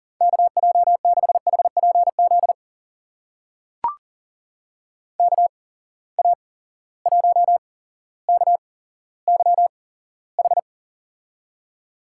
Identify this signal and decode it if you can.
Morse code, heard as a keyed tone, about 38 wpm, spaced as for 17 wpm, 700 Hz. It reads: KJ6HPZ KA1KYH
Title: USA Call Signs - 6 Characters - 8x Character Spacing - 40wpm